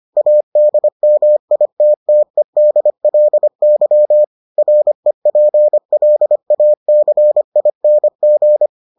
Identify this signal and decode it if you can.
Morse code, heard as a keyed tone, about 25 wpm, 600 Hz. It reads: ADMITTEDLY REPLACING